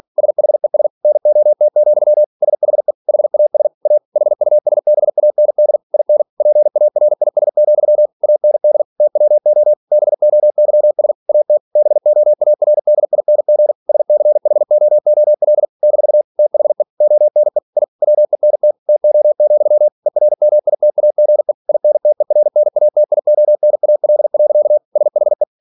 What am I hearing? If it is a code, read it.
Morse code, heard as a keyed tone, about 46 wpm, 600 Hz. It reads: SHES NOT, SHE HAS A HUSBAND IN PARIS, AND TWO BOYS AT BOARDING SCHOOL - THE ONE I WENT TO, ERMITAGE INTERNATIONAL. SHE
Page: Never 47wpm